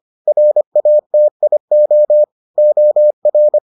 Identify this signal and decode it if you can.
Morse code, heard as a keyed tone, about 25 wpm, 600 Hz. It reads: RATIO OR